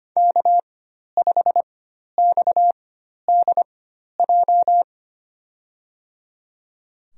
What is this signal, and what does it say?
Morse code, heard as a keyed tone, about 25 wpm, spaced as for 12 wpm, 700 Hz. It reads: K5XDJ
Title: USA Call Signs - 4x Character Spacing - 25wpm